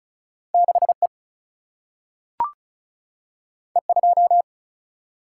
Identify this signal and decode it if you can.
Morse code, heard as a keyed tone, about 36 wpm, 700 Hz. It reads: 6 E E 2
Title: Single-Letter Single-Number 35wpm